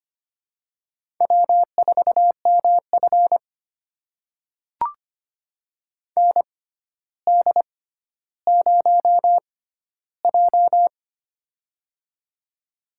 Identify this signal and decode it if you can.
Morse code, heard as a keyed tone, about 25 wpm, spaced as for 13 wpm, 700 Hz. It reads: W4MF ND0J
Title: USA Call Signs - 4 Characters - 6x Character Spacing - 25wpm